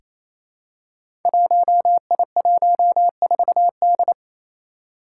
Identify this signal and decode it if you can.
Morse code, heard as a keyed tone, about 27 wpm, 700 Hz. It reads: 1I14D
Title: Five Challenging Letter-Number Combinations 28wpm